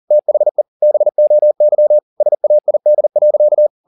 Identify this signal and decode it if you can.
Morse code, heard as a keyed tone, about 40 wpm, 600 Hz. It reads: THE BOY SAID.
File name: the old man and the sea 1 40wpm